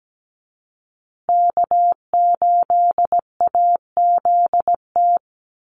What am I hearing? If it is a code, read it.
Morse code, heard as a keyed tone, about 17 wpm, 700 Hz. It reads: K8AZT
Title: Call Signs Repeated 2x - 17wpm